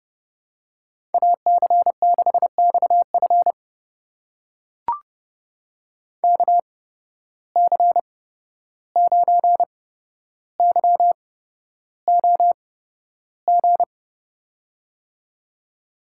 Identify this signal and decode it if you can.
Morse code, heard as a keyed tone, about 31 wpm, spaced as for 12 wpm, 700 Hz. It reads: AC6XF KC9YOG
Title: USA Call Signs - 8x Character Spacing - 30wpm